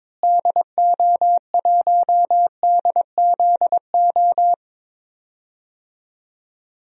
Morse code, heard as a keyed tone, about 22 wpm, 700 Hz. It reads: DO1DZO
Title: DXCC-100 Call Signs - 6 Characters - Repeated 2x - 22wpm